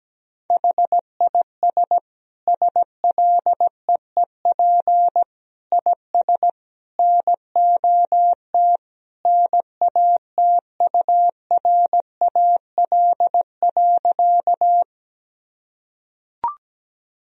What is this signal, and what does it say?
Morse code, heard as a keyed tone, about 17 wpm, 700 Hz. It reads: HIS SLEEP IS NOT NATURAL.
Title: Sentences from Top 1000 Words - Review Part 4 - 17wpm